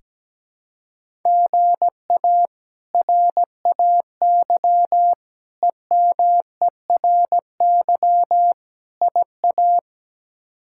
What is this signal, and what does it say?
Morse code, heard as a keyed tone, about 17 wpm, 700 Hz. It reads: GA RAY EMERY IA
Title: K1USN SST QSO Practice - 17wpm